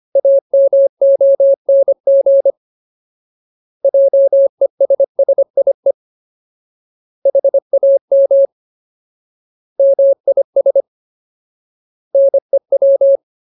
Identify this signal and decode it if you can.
Morse code, heard as a keyed tone, about 25 wpm, 550 Hz. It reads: AMONG JESSIE HAM MIS NEW
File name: Ultimate Word List / 25WPM Ultimate